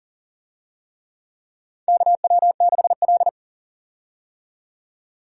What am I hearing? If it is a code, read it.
Morse code, heard as a keyed tone, about 40 wpm, 700 Hz. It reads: KW6L
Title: USA Call Signs - 4 Characters - 40wpm